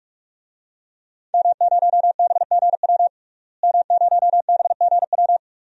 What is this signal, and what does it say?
Morse code, heard as a keyed tone, about 46 wpm, 700 Hz. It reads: M0BGW M0BGW
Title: DXCC-100 Call Signs - 5 Characters - Repeated 2x - 45wpm